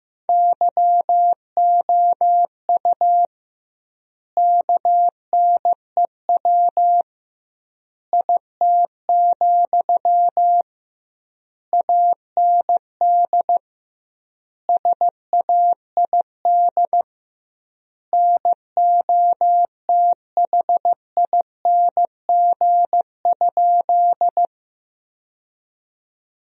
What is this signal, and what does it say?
Morse code, heard as a keyed tone, about 15 wpm, 700 Hz. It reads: YOU KNEW IT, AND SAID NOTHING?
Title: 2x Word-Spacing - Sentences from Top 500 Words - 15wpm